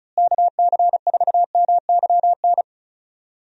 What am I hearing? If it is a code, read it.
Morse code, heard as a keyed tone, about 34 wpm, 700 Hz. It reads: KC4MYN
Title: USA Call Signs - 6 Characters - Repeated 2x - 35wpm